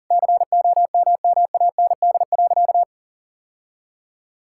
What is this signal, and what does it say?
Morse code, heard as a keyed tone, about 40 wpm, 700 Hz. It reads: COMMAND.
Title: Sentences from Top 1600 Words 40wpm